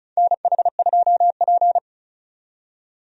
Morse code, heard as a keyed tone, about 34 wpm, 700 Hz. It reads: NH2P
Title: USA Call Signs - 4 Characters - 35wpm